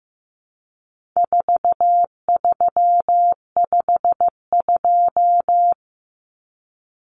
Morse code, heard as a keyed tone, about 15 wpm, 700 Hz. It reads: 4352
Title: Common Cut Numbers - 4-Digit 15wpm